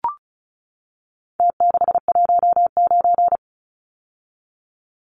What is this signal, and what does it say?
Morse code, heard as a keyed tone, about 34 wpm, 700 Hz. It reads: T619
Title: Numbers - Contest Style - 4-Digit 35wpm